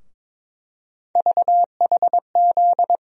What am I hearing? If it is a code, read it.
Morse code, heard as a keyed tone, about 22 wpm, 700 Hz. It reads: VHZ
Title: USA Call Sign Postfixes - 3 Letters 22wpm